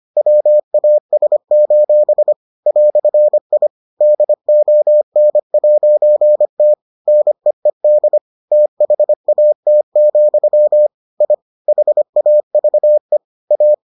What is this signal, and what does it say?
Morse code, heard as a keyed tone, about 25 wpm, 600 Hz. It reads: WAS: "I DON'T NEED THAT, I HAVE A